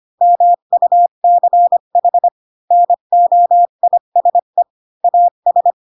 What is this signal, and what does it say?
Morse code, heard as a keyed tone, about 25 wpm, 700 Hz. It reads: MUCH NOISE AS
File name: Pinocchio / Pinocchio ch3 20 25 ik1hge